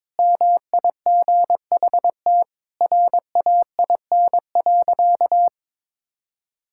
Morse code, heard as a keyed tone, about 22 wpm, 700 Hz. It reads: MIGHT RAIN.
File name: practice / 22wpm / Sentences with 3-Word Phrases - 22wpm